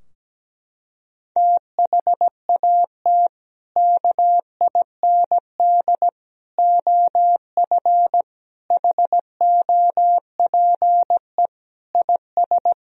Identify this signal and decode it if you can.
Morse code, heard as a keyed tone, about 17 wpm, 700 Hz. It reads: THAT KIND OF HOPE IS
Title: Sentences from Top 1000 Words - Review Part 3 - 17wpm